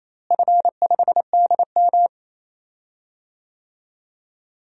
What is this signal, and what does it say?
Morse code, heard as a keyed tone, about 28 wpm, 700 Hz. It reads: F5DM
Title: DXCC-100 Call Signs - 4 Characters - Repeated 3x - 28wpm